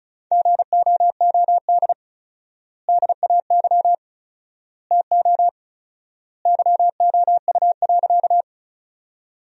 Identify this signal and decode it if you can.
Morse code, heard as a keyed tone, about 36 wpm, 700 Hz. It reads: GOOD DAY TO YOU.
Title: 4x Word-Spacing - Sentences from Top 200 Words - 35wpm